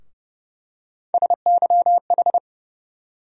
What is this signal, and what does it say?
Morse code, heard as a keyed tone, about 30 wpm, 700 Hz. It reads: SYH
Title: USA Call Sign Postfixes - 3 Letters 30wpm